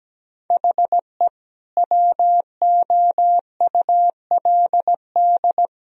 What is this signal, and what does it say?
Morse code, heard as a keyed tone, about 17 wpm, 700 Hz. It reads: HE WOULD
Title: Sentences from Top 1000 Words - Review Part 1 - 17wpm